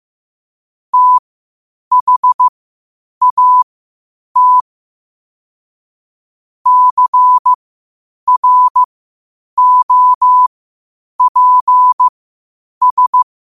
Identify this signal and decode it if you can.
Morse code, heard as a keyed tone, about 15 wpm, spaced as for 8 wpm, 1000 Hz. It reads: THAT CROPS
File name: qotdfiles / 8wpm